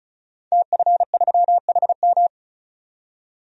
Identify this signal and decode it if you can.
Morse code, heard as a keyed tone, about 36 wpm, 700 Hz. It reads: TF3HM
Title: DXCC-100 Call Signs - 5 Characters - 35wpm